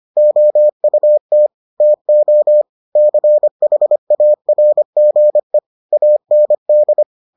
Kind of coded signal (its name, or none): Morse code